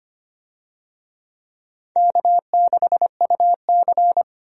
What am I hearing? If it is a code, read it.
Morse code, heard as a keyed tone, about 25 wpm, 700 Hz. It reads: K6UC
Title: USA Call Signs - 4 Characters - 25wpm